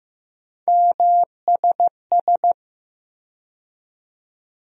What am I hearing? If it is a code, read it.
Morse code, heard as a keyed tone, about 15 wpm, 700 Hz. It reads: MSS
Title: USA Call Sign Postfixes - 3 Letters 15wpm